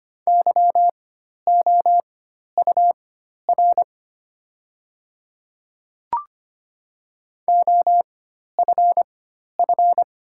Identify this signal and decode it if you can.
Morse code, heard as a keyed tone, about 25 wpm, spaced as for 12 wpm, 700 Hz. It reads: YOUR OFF
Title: 4x Character Spacing - Top 200 Words - 25wpm